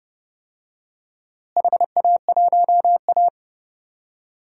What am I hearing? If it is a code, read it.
Morse code, heard as a keyed tone, about 30 wpm, 700 Hz. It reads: HA1A
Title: DXCC-100 Call Signs - 4 Characters - Repeated 2x - 30wpm